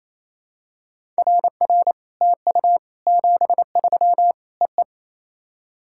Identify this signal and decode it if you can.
Morse code, heard as a keyed tone, about 27 wpm, 700 Hz. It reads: RR TU 73 EE
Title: SOTA QSO Practice 28wpm